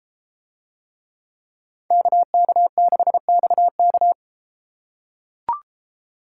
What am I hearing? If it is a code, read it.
Morse code, heard as a keyed tone, about 31 wpm, 700 Hz. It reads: KK6XK